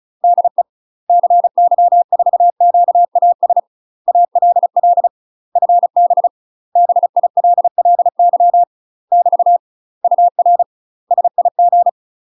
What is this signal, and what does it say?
Morse code, heard as a keyed tone, about 36 wpm, 700 Hz. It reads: DE CY4QAS ALL FB BILLY <BT> UR SIG
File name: UK MASTER QSOs TRAINING/35WPM